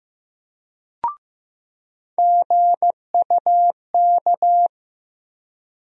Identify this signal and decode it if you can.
Morse code, heard as a keyed tone, about 15 wpm, 700 Hz. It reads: GUK
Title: USA Call Sign Postfixes - 3 Letters 15wpm